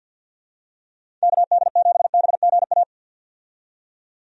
Morse code, heard as a keyed tone, about 51 wpm, 700 Hz. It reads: KD7BGA
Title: USA Call Signs - 6 Characters - 50wpm